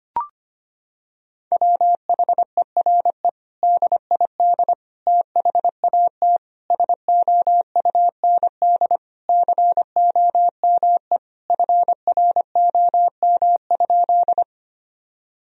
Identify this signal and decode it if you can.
Morse code, heard as a keyed tone, about 25 wpm, 700 Hz. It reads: WHERE DID THAT SOUND COME FROM?
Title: Sentences from Top 1000 Words - Review Part 1 - 25wpm